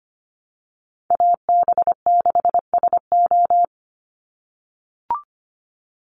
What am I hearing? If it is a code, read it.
Morse code, heard as a keyed tone, about 25 wpm, 700 Hz. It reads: AB6SO
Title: NA SOTA Chaser Callsigns 25wpm